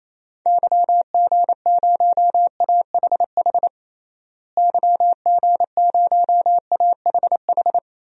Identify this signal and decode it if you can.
Morse code, heard as a keyed tone, about 28 wpm, 700 Hz. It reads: YG0AHH YG0AHH
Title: DXCC-100 Call Signs - 6 Characters - Repeated 3x - 28wpm